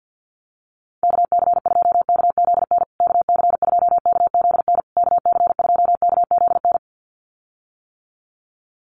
Morse code, heard as a keyed tone, about 48 wpm, 700 Hz. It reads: KC2KZN KC2KZN KC2KZN
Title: USA Call Signs Repeated 3x 50wpm